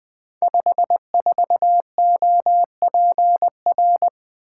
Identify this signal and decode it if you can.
Morse code, heard as a keyed tone, about 20 wpm, 700 Hz. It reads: 54OPR
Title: Code Groups - 5 Letter:Numbers - 20wpm